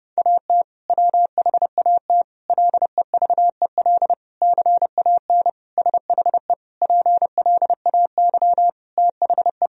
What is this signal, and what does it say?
Morse code, heard as a keyed tone, about 30 wpm, 700 Hz. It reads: AT WHAT LEVEL CAN SHE PLAY THE
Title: Sentences from Top 1000 Words - Review Part 4 - 30wpm